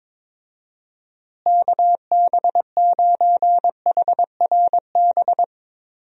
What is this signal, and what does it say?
Morse code, heard as a keyed tone, about 21 wpm, 700 Hz. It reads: KB9HRB
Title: USA Call Signs - 6 Characters - Repeated 2x - 22wpm